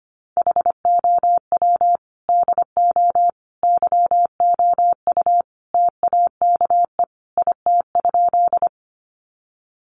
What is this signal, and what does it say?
Morse code, heard as a keyed tone, about 25 wpm, 700 Hz. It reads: HOW DO YOU TAKE IT?
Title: Sentences from Top 200 Words 25wpm